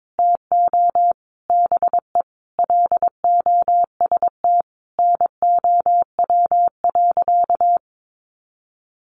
Morse code, heard as a keyed tone, about 22 wpm, 700 Hz. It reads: TO BE LOST NOW.
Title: Sentences from Top 1400 Words 22wpm